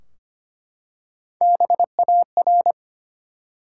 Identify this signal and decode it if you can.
Morse code, heard as a keyed tone, about 25 wpm, 700 Hz. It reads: BAR
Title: Three Letter Words - 25wpm